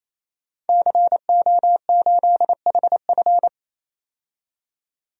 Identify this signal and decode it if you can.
Morse code, heard as a keyed tone, about 28 wpm, 700 Hz. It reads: CO8HF
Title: DXCC-100 Call Signs - 5 Characters - 28wpm